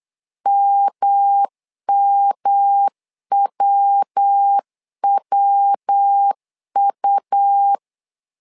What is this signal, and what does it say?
Morse code, heard as a keyed tone, about 8 wpm, 800 Hz. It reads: MMWWU
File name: LEZIONI / 3 MWRO / 3 MW 8wpm 2501